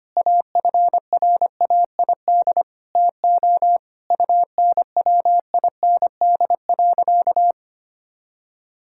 Morse code, heard as a keyed tone, about 25 wpm, 700 Hz. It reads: AFRAID TO UNWIND.
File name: practice / 25wpm / All-Age Appropriate Jokes - 25wpm